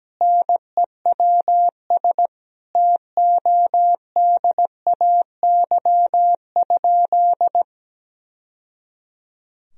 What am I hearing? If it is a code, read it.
Morse code, heard as a keyed tone, about 17 wpm, 700 Hz. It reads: NEWS TODAY?
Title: Sentences from Top 1000 Words - Review Part 4 - 17wpm